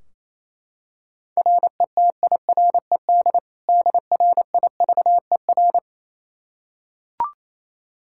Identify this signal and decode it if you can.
Morse code, heard as a keyed tone, about 28 wpm, 700 Hz. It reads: RETIRED DRIVER
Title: Practice QSO - Exchange 3 - More Info - 28wpm